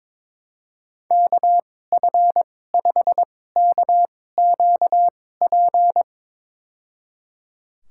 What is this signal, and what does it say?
Morse code, heard as a keyed tone, about 22 wpm, spaced as for 16 wpm, 700 Hz. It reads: KF5KQP
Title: USA Call Signs - 6 Characters - 2x Character Spacing - 22wpm